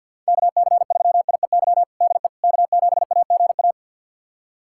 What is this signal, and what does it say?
Morse code, heard as a keyed tone, about 51 wpm, 700 Hz. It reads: KC3IEY DE K7AGA
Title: USA Callsign DE Callsign 50wpm